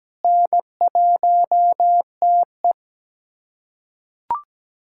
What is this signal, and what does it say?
Morse code, heard as a keyed tone, about 17 wpm, 700 Hz. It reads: N1TE
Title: USA Callsign DE Callsign - 4 Characters - 17wpm